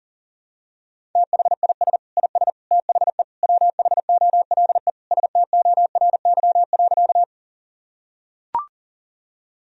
Morse code, heard as a keyed tone, about 40 wpm, 700 Hz. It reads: THIS IS THE WHOLE STORY.
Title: Sentences from Top 1000 Words - Review Part 1 - 40wpm